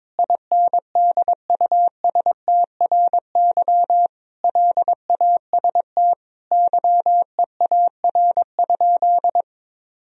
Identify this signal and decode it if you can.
Morse code, heard as a keyed tone, about 21 wpm, 700 Hz. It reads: INDUSTRY LAST YEAR?
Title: Sentences from Top 1000 Words - Review Part 4 - 22wpm